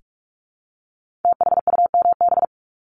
Morse code, heard as a keyed tone, about 46 wpm, 700 Hz. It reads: THUMB
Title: Multi-letter Phonemes Repeated 3x 45wpm